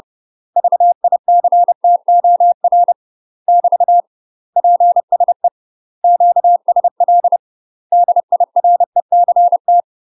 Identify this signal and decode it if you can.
Morse code, heard as a keyed tone, about 30 wpm, 700 Hz. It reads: VICTOR <BT> PSE QSL DIRECT
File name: UK MASTER QSOs TRAINING/30WPM